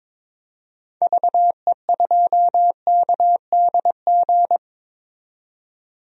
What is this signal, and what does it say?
Morse code, heard as a keyed tone, about 21 wpm, 700 Hz. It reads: VE2KDG
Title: DXCC-100 Call Signs - 6 Characters - 22wpm